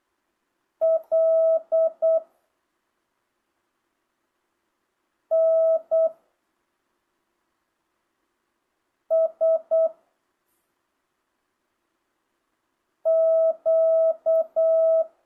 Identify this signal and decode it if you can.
Morse code, heard as a keyed tone, about 8 wpm, 660 Hz. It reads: L N S Q
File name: jh8cbh / 2025cw